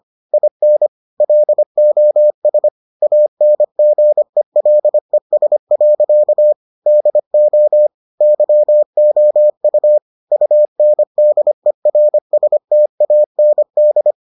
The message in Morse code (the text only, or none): IN LOS ANGELES. DO YOU UNDERSTAND